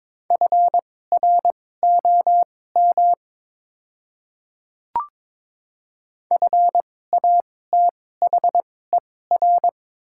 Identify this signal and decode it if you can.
Morse code, heard as a keyed tone, about 22 wpm, spaced as for 16 wpm, 700 Hz. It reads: FROM FATHER
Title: 2x Character Spacing - Top 200 Words - 22wpm